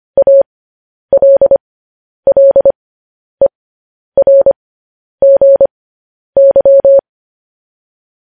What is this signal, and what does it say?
Morse code, heard as a keyed tone, about 25 wpm, spaced as for 10 wpm, 550 Hz. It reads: ALLERGY
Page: news headlines in morse code at 10 wpm